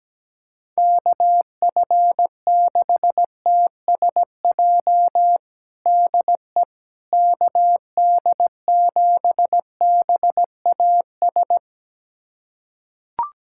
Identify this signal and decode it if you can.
Morse code, heard as a keyed tone, about 17 wpm, 700 Hz. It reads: KF6TSJ DE KD7BAS
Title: USA Callsign DE Callsign - 6 Characters - 17wpm